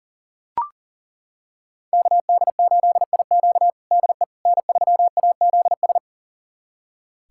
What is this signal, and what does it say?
Morse code, heard as a keyed tone, about 40 wpm, 700 Hz. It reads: KD8IQ DE N3AZS
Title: USA Callsign DE Callsign - 5 Characters - 40wpm